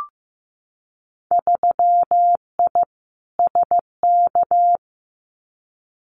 Morse code, heard as a keyed tone, about 15 wpm, 700 Hz. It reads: 3I SK
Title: Winter Field Day Exchanges Encore 15wpm